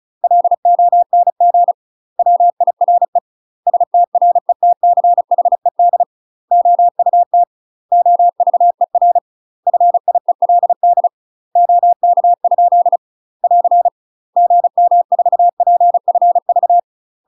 Morse code, heard as a keyed tone, about 36 wpm, 700 Hz. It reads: LONG WIRE STRETCHED OUT OVER FIELD OK? <AR> GM4PFV
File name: UK MASTER QSOs TRAINING / 35WPM / UK MASTER QSOs 0014